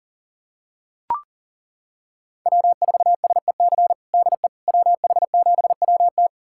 Morse code, heard as a keyed tone, about 37 wpm, 700 Hz. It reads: W4SEC DE WH7WT
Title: USA Callsign DE Callsign - 5 Characters - 40wpm